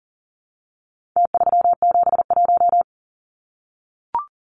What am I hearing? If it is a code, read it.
Morse code, heard as a keyed tone, about 40 wpm, 700 Hz. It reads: T371
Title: Numbers - Contest Style - 4-Digit 40wpm